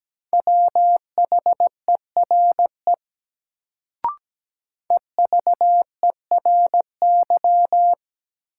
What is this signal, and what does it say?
Morse code, heard as a keyed tone, about 17 wpm, 700 Hz. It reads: WHERE EVERY
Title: Sentences from Top 200 Words - New Word Drill - 17wpm